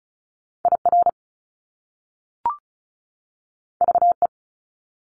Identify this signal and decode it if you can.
Morse code, heard as a keyed tone, about 35 wpm, 700 Hz. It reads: IR VE
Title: Top Two Letter Combinations in Words 35wpm